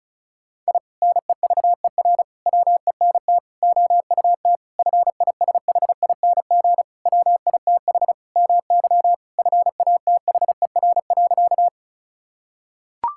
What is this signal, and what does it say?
Morse code, heard as a keyed tone, about 34 wpm, 700 Hz. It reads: I NEVER WENT OUT FISHING WITH MY FATHER.
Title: Sentences from Top 1000 Words - Review Part 5 - 35wpm